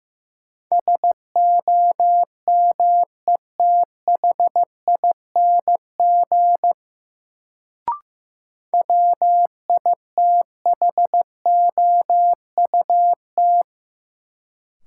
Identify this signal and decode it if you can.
Morse code, heard as a keyed tone, about 15 wpm, 700 Hz. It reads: SOMETHING WITHOUT
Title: Sentences from Top 600 Words - New Word Drill - 15wpm